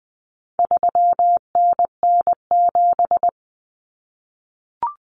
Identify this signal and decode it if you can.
Morse code, heard as a keyed tone, about 20 wpm, 700 Hz. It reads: 3NN7
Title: Common Cut Numbers - 4-Digit 20wpm